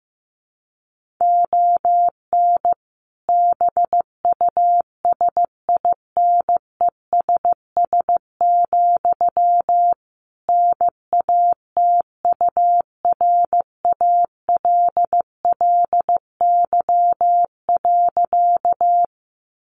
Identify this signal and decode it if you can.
Morse code, heard as a keyed tone, about 15 wpm, 700 Hz. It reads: ON BUSINESS, NATURALLY.
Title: Sentences from Top 1300 Words 15wpm